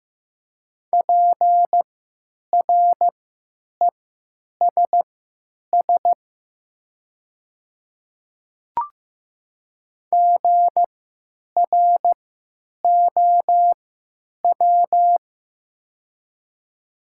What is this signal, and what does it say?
Morse code, heard as a keyed tone, about 15 wpm, spaced as for 9 wpm, 700 Hz. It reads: PRESS GROW
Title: 3x Character Spacing - Top 300 Words - 15wpm